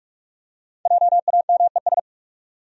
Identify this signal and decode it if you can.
Morse code, heard as a keyed tone, about 46 wpm, 700 Hz. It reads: JAMES
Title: USA Common Names - 45wpm